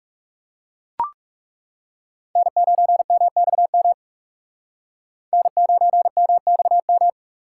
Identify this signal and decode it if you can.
Morse code, heard as a keyed tone, about 37 wpm, 700 Hz. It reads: N9MXM N9MXM
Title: Speed Racing - USA 1x3 Call Signs 30wpm